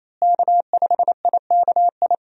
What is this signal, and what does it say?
Morse code, heard as a keyed tone, about 28 wpm, 700 Hz. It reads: K5IKI
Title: USA Call Signs - 5 Characters - Repeated 3x - 28wpm